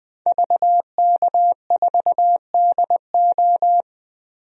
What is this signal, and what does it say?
Morse code, heard as a keyed tone, about 20 wpm, 700 Hz. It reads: VK4DO
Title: DXCC-100 Call Signs - 5 Characters - 20wpm